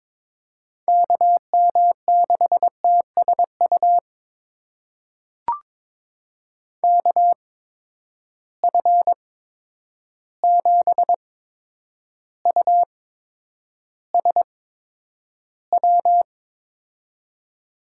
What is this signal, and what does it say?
Morse code, heard as a keyed tone, about 22 wpm, spaced as for 9 wpm, 700 Hz. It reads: KM6TSU KF7USW
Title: USA Call Signs - 6 Characters - 8x Character Spacing - 22wpm